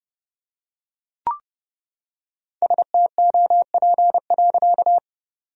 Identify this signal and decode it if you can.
Morse code, heard as a keyed tone, about 30 wpm, 700 Hz. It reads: STOP.
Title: Sentences from Top 1000 Words - Review Part 1 - 30wpm